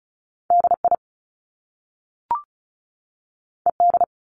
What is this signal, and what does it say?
Morse code, heard as a keyed tone, about 36 wpm, 700 Hz. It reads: DI ED
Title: Top Two Letter Combinations in Words 35wpm